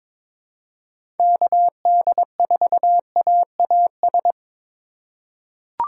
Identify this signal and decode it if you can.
Morse code, heard as a keyed tone, about 22 wpm, 700 Hz. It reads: KD4AAS
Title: USA Call Signs - 4x Character Spacing - 22wpm